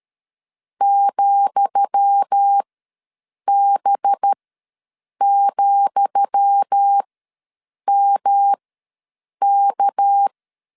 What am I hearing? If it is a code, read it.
Morse code, heard as a keyed tone, about 13 wpm, spaced as for 7 wpm, 800 Hz. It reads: ,B,MK
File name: LEZIONI / 10 SIGNS / 10D COMA ws 12wpm 3618